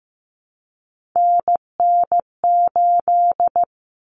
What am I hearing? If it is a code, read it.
Morse code, heard as a keyed tone, about 15 wpm, 700 Hz. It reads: NN8
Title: Common Cut Numbers - 3-Digit 15wpm